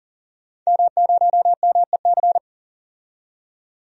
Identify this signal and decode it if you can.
Morse code, heard as a keyed tone, about 40 wpm, 700 Hz. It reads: M0MEC
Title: DXCC-100 Call Signs - 5 Characters - 40wpm